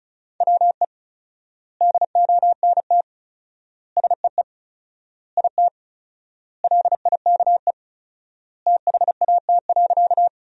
Morse code, heard as a keyed tone, about 34 wpm, 700 Hz. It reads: WE DONT SEE IT LIKE THAT.
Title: 4x Word-Spacing - Sentences from Top 300 Words - 35wpm